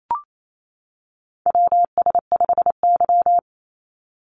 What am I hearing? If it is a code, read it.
Morse code, heard as a keyed tone, about 28 wpm, 700 Hz. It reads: WS5Y
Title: USA Call Signs 28wpm